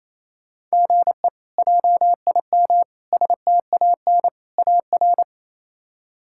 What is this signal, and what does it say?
Morse code, heard as a keyed tone, about 28 wpm, 700 Hz. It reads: GE JIM STAN AR
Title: K1USN SST QSO Practice - 28wpm